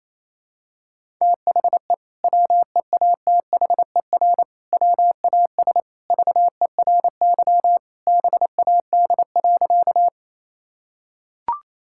Morse code, heard as a keyed tone, about 28 wpm, 700 Hz. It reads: THE WEATHER WAS VERY BAD.
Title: Sentences from Top 1000 Words - Review Part 5 - 28wpm